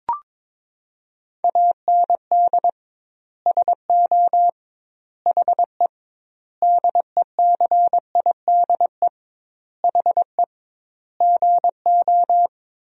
Morse code, heard as a keyed tone, about 22 wpm, 700 Hz. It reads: AND SO HE DECIDE HE GO
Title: 2x Word-Spacing - Sentences from Top 500 Words - 22wpm